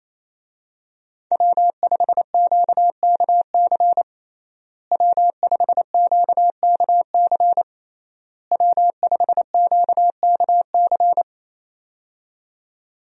Morse code, heard as a keyed tone, about 28 wpm, 700 Hz. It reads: W5QKC W5QKC W5QKC
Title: USA Call Signs - 5 Characters - Repeated 3x - 28wpm